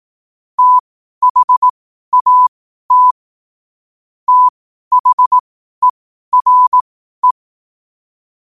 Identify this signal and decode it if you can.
Morse code, heard as a keyed tone, about 18 wpm, spaced as for 12 wpm, 1000 Hz. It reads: THAT THERE
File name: qotdfiles / 12wpm